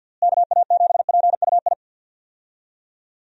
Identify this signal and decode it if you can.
Morse code, heard as a keyed tone, about 47 wpm, 700 Hz. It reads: KA7PUI
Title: USA Call Signs - 6 Characters - Repeated 3x - 50wpm